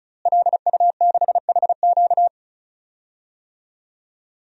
Morse code, heard as a keyed tone, about 34 wpm, 700 Hz. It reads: LU6HQ
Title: DXCC-100 Call Signs - 5 Characters - Repeated 2x - 35wpm